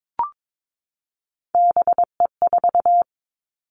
Morse code, heard as a keyed tone, about 22 wpm, 700 Hz. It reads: BE4
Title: Three Challenging Letter-Number Combinations 22wpm